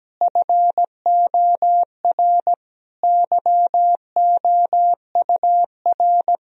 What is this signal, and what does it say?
Morse code, heard as a keyed tone, about 17 wpm, 700 Hz. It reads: FOR YOUR
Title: Sentences from Top 1000 Words - Review Part 3 - 17wpm